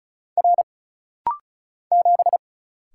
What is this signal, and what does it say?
Morse code, heard as a keyed tone, about 36 wpm, 700 Hz. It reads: R 7
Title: Single Letter-Number - Rapid-Fire 35wpm